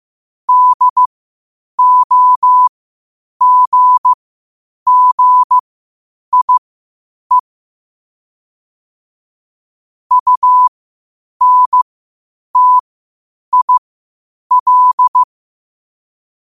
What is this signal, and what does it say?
Morse code, heard as a keyed tone, about 15 wpm, spaced as for 8 wpm, 1000 Hz. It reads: DOGGIE UNTIL
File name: qotdfiles / 8wpm